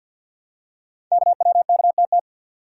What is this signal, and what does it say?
Morse code, heard as a keyed tone, about 52 wpm, 700 Hz. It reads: KWKTT
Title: Code Groups - 5 Letters - 50wpm